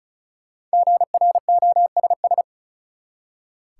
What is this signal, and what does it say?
Morse code, heard as a keyed tone, about 34 wpm, 700 Hz. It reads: GROSS
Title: Five Letter Words - 35wpm